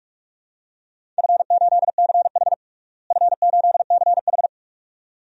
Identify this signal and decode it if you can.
Morse code, heard as a keyed tone, about 46 wpm, 700 Hz. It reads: F8CH F8CH
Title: DXCC-100 Call Signs - 4 Characters - Repeated 2x - 45wpm